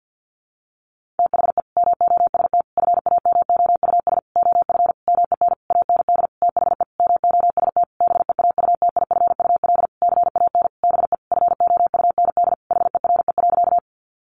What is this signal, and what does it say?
Morse code, heard as a keyed tone, about 50 wpm, 700 Hz. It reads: THE MOST FAMOUS OF MEN AND THE MOST BEAUTIFUL CAN BE FOUND HERE.
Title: Sentences from Top 1500 Words 50wpm